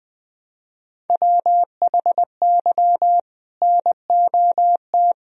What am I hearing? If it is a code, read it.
Morse code, heard as a keyed tone, about 20 wpm, 700 Hz. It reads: WHY NOT
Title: Sentences from Top 1000 Words - Review Part 1 - 20wpm